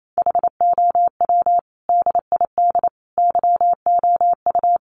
Morse code, heard as a keyed tone, about 28 wpm, 700 Hz. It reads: HOW DID YOU
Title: Sentences from Top 100 Words 28wpm